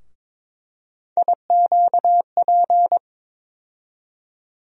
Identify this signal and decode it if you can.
Morse code, heard as a keyed tone, about 22 wpm, 700 Hz. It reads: IQP
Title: USA Call Sign Postfixes - 3 Letters 22wpm